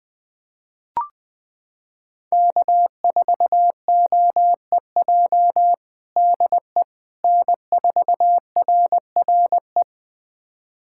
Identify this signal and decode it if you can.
Morse code, heard as a keyed tone, about 20 wpm, 700 Hz. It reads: K4OEJ DE N4RRE
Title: USA Callsign DE Callsign - 5 Characters - 20wpm